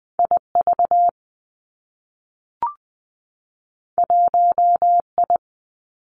Morse code, heard as a keyed tone, about 20 wpm, 700 Hz. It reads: I V 1 I
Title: Two Challenging Letter-Number Combinations 20wpm